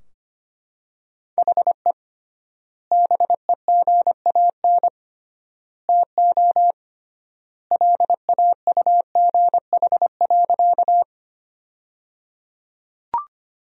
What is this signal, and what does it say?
Morse code, heard as a keyed tone, about 25 wpm, 700 Hz. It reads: HE BEGAN TO LAUGH.
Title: 3x Word-Spacing - Sentences from Top 500 Words - 25wpm